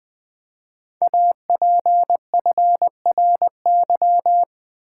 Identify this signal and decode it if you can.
Morse code, heard as a keyed tone, about 20 wpm, 700 Hz. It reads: APFRY
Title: Code Groups - 5 Letters - 20wpm